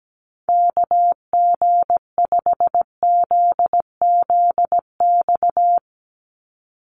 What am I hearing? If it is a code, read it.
Morse code, heard as a keyed tone, about 17 wpm, 700 Hz. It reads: KG5ZZX
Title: USA Call Signs 17wpm